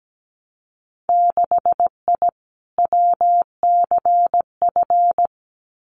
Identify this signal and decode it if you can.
Morse code, heard as a keyed tone, about 17 wpm, 700 Hz. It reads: 6I WCF
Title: Winter Field Day Exchanges Encore 17wpm